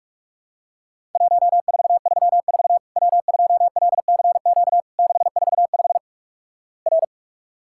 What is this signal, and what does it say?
Morse code, heard as a keyed tone, about 46 wpm, 700 Hz. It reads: 1434 W2LCQ 645 R
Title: Worked All Europe DX Contest - QTC Exchange - 45wpm